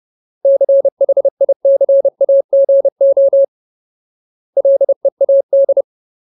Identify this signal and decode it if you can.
Morse code, heard as a keyed tone, about 30 wpm, 550 Hz. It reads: CHICAGO LEAD
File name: Ultimate Word List / 30WPM Ultimate